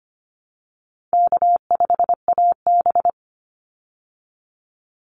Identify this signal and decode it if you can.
Morse code, heard as a keyed tone, about 25 wpm, 700 Hz. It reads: K5AB
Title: Easy Call Signs Repeated 3x - 25wpm